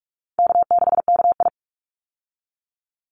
Morse code, heard as a keyed tone, about 44 wpm, 700 Hz. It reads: K6KI
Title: Call Signs Repeated 2x - 45wpm